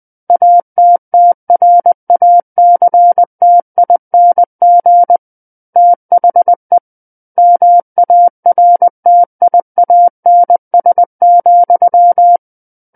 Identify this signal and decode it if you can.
Morse code, heard as a keyed tone, about 20 wpm, 700 Hz. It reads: ATTRACTING THE MARTIANS,
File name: 20 WPM